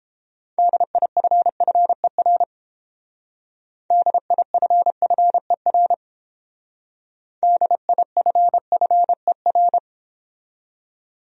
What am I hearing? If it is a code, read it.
Morse code, heard as a keyed tone, about 29 wpm, 700 Hz. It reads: DIFFER DIFFER DIFFER